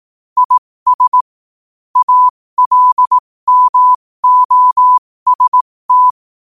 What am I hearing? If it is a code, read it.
Morse code, heard as a keyed tone, about 18 wpm, spaced as for 15 wpm, 1000 Hz. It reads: IS ALMOST
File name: qotdfiles / 15wpm